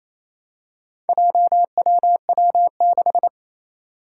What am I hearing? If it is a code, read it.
Morse code, heard as a keyed tone, about 28 wpm, 700 Hz. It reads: JWW6
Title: Four Challenging Letter-Number Combinations 28wpm